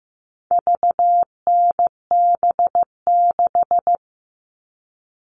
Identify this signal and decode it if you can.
Morse code, heard as a keyed tone, about 15 wpm, 700 Hz. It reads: VNB6
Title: All Cut Numbers - 4-Digit 15wpm